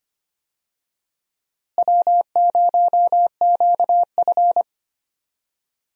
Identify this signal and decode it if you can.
Morse code, heard as a keyed tone, about 25 wpm, 700 Hz. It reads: W0QF
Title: USA Call Signs - 4 Characters - Repeated 2x - 25wpm